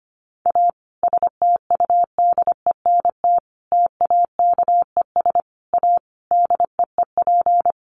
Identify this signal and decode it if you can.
Morse code, heard as a keyed tone, about 25 wpm, 700 Hz. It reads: A STUDENT TAKES A DEEP
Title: Additional Easy Phrases 25wpm